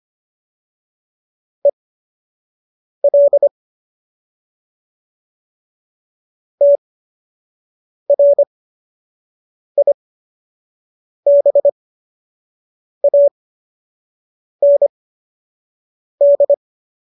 Morse code, heard as a keyed tone, about 25 wpm, spaced as for 6 wpm, 570 Hz. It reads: EL TRIBAND
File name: qso012 06